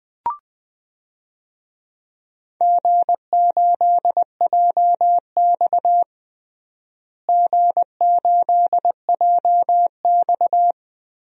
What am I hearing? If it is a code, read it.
Morse code, heard as a keyed tone, about 20 wpm, 700 Hz. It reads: G8JX G8JX
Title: DXCC-100 Call Signs - 4 Characters - Repeated 2x - 20wpm